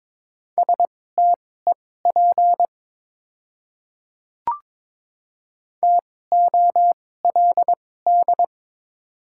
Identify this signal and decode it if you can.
Morse code, heard as a keyed tone, about 22 wpm, spaced as for 16 wpm, 700 Hz. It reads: STEP TOLD
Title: 2x Character Spacing - Top 500 Words - 22wpm